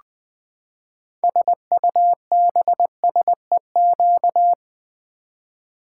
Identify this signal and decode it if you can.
Morse code, heard as a keyed tone, about 20 wpm, 700 Hz. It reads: SUBSEQ
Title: Build A Word - 6 Letters At A Time - 12-Letter Words - 20wpm